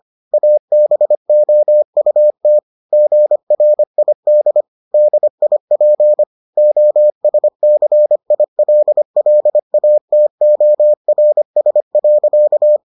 Morse code, heard as a keyed tone, about 25 wpm, 600 Hz. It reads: ABOUT GRID DIP OSCILLATORS.